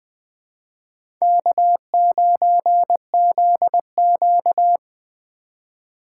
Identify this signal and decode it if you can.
Morse code, heard as a keyed tone, about 20 wpm, 700 Hz. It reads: K9ZQ
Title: USA Call Signs - 4 Characters - 3x Character Spacing - 20wpm